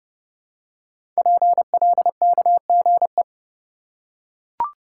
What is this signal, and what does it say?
Morse code, heard as a keyed tone, about 30 wpm, 700 Hz. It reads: PLKGE
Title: Code Groups - 5 Letters - 30wpm